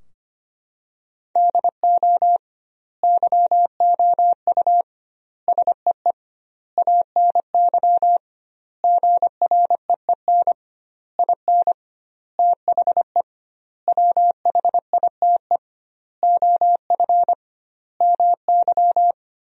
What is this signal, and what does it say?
Morse code, heard as a keyed tone, about 25 wpm, 700 Hz. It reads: DO YOU SEE ANY GREEN IN THE WHITE OF MY
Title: 2x Word-Spacing - Sentences from Top 500 Words - 25wpm